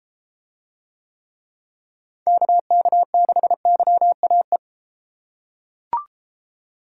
Morse code, heard as a keyed tone, about 32 wpm, 700 Hz. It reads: KK6YAE